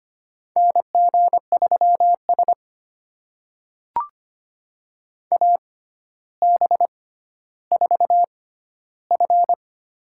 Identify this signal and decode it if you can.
Morse code, heard as a keyed tone, about 25 wpm, spaced as for 13 wpm, 700 Hz. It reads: NG3S AB4F
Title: USA Call Signs - 4 Characters - 6x Character Spacing - 25wpm